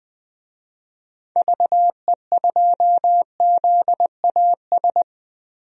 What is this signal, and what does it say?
Morse code, heard as a keyed tone, about 20 wpm, 700 Hz. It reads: VE2ZAS
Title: DXCC-100 Call Signs - 6 Characters - Repeated 2x - 20wpm